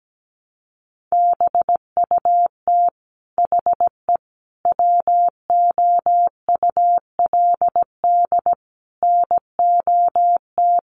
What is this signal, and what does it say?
Morse code, heard as a keyed tone, about 17 wpm, 700 Hz. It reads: BUT HE WOULD NOT
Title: Sentences from Top 1100 Words 17wpm